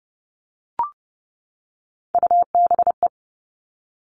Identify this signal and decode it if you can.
Morse code, heard as a keyed tone, about 30 wpm, 700 Hz. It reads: UBE
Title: All Cut Numbers - 3-Digit 30wpm